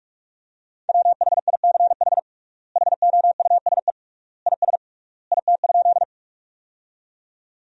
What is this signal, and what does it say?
Morse code, heard as a keyed tone, about 46 wpm, 700 Hz. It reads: WHICH HOUSE IS IT?
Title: 3x Word-Spacing - Sentences from Top 200 Words - 45wpm